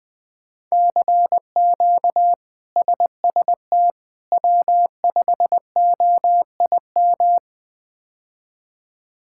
Morse code, heard as a keyed tone, about 21 wpm, 700 Hz. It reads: CQ SST W5OIM
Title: K1USN SST QSO Practice - 20wpm